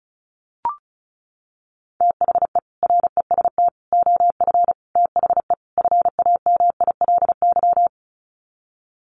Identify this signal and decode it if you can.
Morse code, heard as a keyed tone, about 34 wpm, 700 Hz. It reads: THE REST OF THE FAMILY
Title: Sets of 5 Words Encore2 35wpm